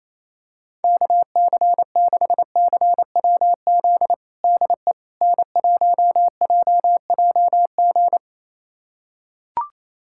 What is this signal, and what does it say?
Morse code, heard as a keyed tone, about 28 wpm, 700 Hz. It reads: KC6CWZ DE N1JJG
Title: USA Callsign DE Callsign 28wpm